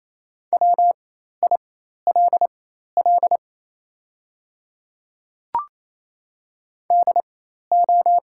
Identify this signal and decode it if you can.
Morse code, heard as a keyed tone, about 28 wpm, spaced as for 13 wpm, 690 Hz. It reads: WILL DO
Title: 4x Character Spacing - Top 100 Words - 28wpm